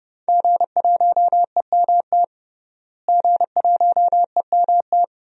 Morse code, heard as a keyed tone, about 30 wpm, 700 Hz. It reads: G1EMT G1EMT
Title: DXCC-100 Call Signs - 5 Characters - Repeated 2x - 30wpm